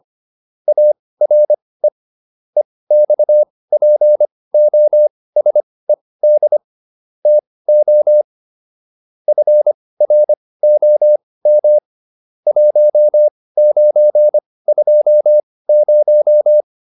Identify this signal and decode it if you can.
Morse code, heard as a keyed tone, about 25 wpm, spaced as for 18 wpm, 600 Hz. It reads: ARE EXPOSED TO FROM 1920